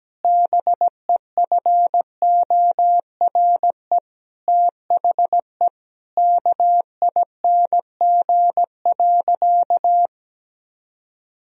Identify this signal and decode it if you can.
Morse code, heard as a keyed tone, about 17 wpm, 700 Hz. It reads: BEFORE THE KING.
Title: Sentences from Top 1000 Words - Review Part 3 - 17wpm